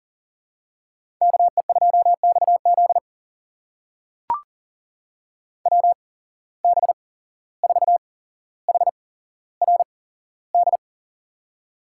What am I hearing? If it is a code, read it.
Morse code, heard as a keyed tone, about 40 wpm, spaced as for 16 wpm, 700 Hz. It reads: KE2XZ WB4HRD
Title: USA Call Signs - 8x Character Spacing - 40wpm